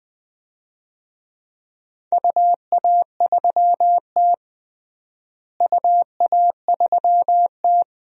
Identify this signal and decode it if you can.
Morse code, heard as a keyed tone, about 19 wpm, 700 Hz. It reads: UA3T UA3T
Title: DXCC-100 Call Signs - 4 Characters - Repeated 3x - 20wpm